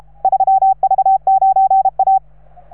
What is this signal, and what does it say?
Morse code, heard as a keyed tone, about 34 wpm, 780 Hz. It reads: 3V9A